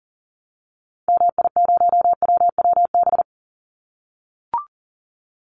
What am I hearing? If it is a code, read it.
Morse code, heard as a keyed tone, about 40 wpm, 700 Hz. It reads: MI0WWB
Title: Call Signs - 40wpm